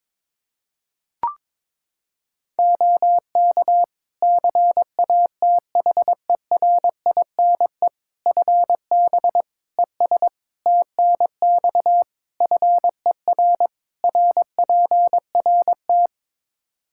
Morse code, heard as a keyed tone, about 22 wpm, 700 Hz. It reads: OK CATHERINE FB ES TNX FER RPRT
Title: Practice QSO - Exchange 2 - Info - 22wpm